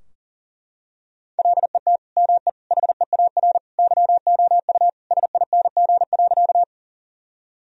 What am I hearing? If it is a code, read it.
Morse code, heard as a keyed tone, about 40 wpm, 700 Hz. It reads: LET ME HEAR YOU SING.
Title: Sentences from Top 1000 Words - Review Part 2 - 40wpm